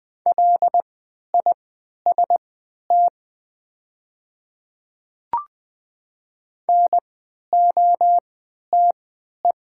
Morse code, heard as a keyed tone, about 20 wpm, spaced as for 11 wpm, 700 Hz. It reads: LIST NOTE
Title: 3x Character Spacing - Top 500 Words - 20wpm